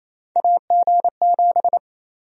Morse code, heard as a keyed tone, about 28 wpm, 700 Hz. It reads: AG7
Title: USA Call Sign Prefixes - 28wpm